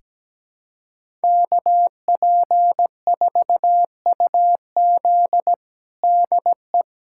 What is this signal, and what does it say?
Morse code, heard as a keyed tone, about 17 wpm, 700 Hz. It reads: KP4UZ DE
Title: USA Callsign DE Callsign - 5 Characters - 17wpm